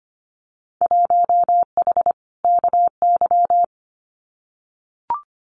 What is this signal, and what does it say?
Morse code, heard as a keyed tone, about 25 wpm, 700 Hz. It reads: 1H KY
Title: Winter Field Day Exchanges 25wpm